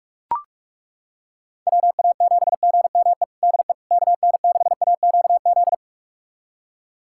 Morse code, heard as a keyed tone, about 42 wpm, 700 Hz. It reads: WA7GME DE KN6AQZ
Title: USA Callsign DE Callsign 45wpm